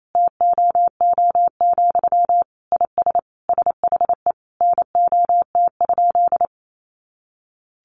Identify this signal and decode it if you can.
Morse code, heard as a keyed tone, about 28 wpm, 700 Hz. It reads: TOO, IS SHE NOT?
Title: Sentences from Top 1300 Words 28wpm